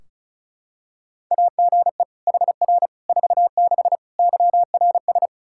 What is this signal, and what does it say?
Morse code, heard as a keyed tone, about 36 wpm, 700 Hz. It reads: AGE HR 46 YRS
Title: Practice QSO - Exchange 3 - More Info - 35wpm